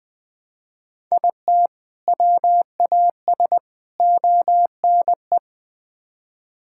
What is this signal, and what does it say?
Morse code, heard as a keyed tone, about 20 wpm, 700 Hz. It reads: IT WAS ONE
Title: Build A Phrase - 5-Word Phrases - 20wpm